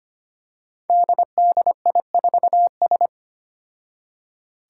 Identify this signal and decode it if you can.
Morse code, heard as a keyed tone, about 25 wpm, 700 Hz. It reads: DDI4S
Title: Five Challenging Letter-Number Combinations 25wpm